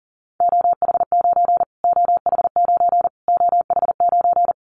Morse code, heard as a keyed tone, about 40 wpm, 700 Hz. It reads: OH9 OH9 OH9
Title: Callsign Prefixes for Top 25 DXCC - Repeated 3x - 40wpm